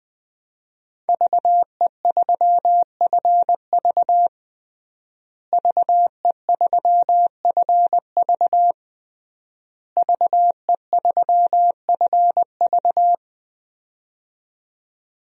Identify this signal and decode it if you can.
Morse code, heard as a keyed tone, about 21 wpm, 700 Hz. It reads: VE3FV VE3FV VE3FV
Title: DXCC-100 Call Signs - 5 Characters - Repeated 3x - 20wpm